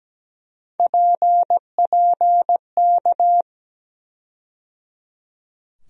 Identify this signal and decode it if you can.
Morse code, heard as a keyed tone, about 17 wpm, 700 Hz. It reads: PPK
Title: USA Call Sign Postfixes - 3 Letters 17wpm